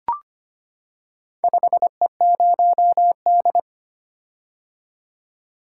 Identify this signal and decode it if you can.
Morse code, heard as a keyed tone, about 25 wpm, 700 Hz. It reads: 5E0D
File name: practice / 25wpm / Four Letter-Number Combinations - 25wpm